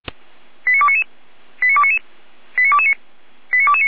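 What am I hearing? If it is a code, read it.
Five-tone selective call (ZVEI), heard as quick runs of five tones, tones of about 70 ms. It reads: 89299 89299 89229 89229
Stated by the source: ZVEI five-tone selective call received via SDR, from the audio archive of the Feuerwehr Orlamünde fire brigade.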